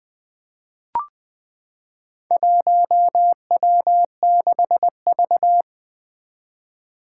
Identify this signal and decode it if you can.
Morse code, heard as a keyed tone, about 20 wpm, 700 Hz. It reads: 1W6V
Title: Four Challenging Letter-Number Combinations 20wpm